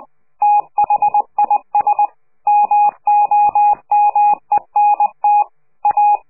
Morse code, heard as a keyed tone, about 20 wpm, 830 Hz. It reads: THIS MOMENT A
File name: CQ de PCH laatste bericht farewell